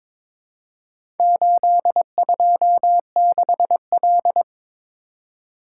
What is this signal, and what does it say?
Morse code, heard as a keyed tone, about 22 wpm, 690 Hz. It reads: 826L
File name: practice / 22wpm / Four Letter-Number Combinations - 22wpm